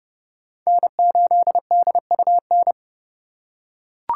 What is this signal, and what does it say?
Morse code, heard as a keyed tone, about 30 wpm, 700 Hz. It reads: N8DUN
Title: USA Call Signs - 5 Characters - 3x Character Spacing - 30wpm